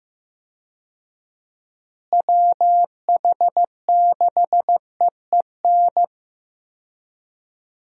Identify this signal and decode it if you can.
Morse code, heard as a keyed tone, about 15 wpm, 700 Hz. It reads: WH6EEN
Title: USA Call Signs - 6 Characters - Repeated 2x - 15wpm